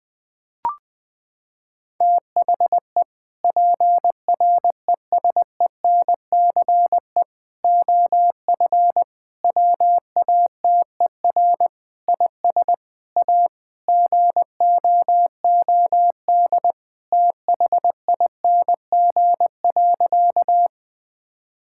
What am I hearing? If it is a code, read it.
Morse code, heard as a keyed tone, about 21 wpm, 700 Hz. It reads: THE PRESENCE OF WATER IS A GOOD THING.
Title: Sentences with 3-Word Phrases - 20wpm